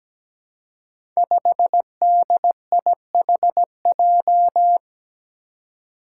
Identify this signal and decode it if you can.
Morse code, heard as a keyed tone, about 17 wpm, 700 Hz. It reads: 5DIHJ
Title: Five Challenging Letter-Number Combinations 17wpm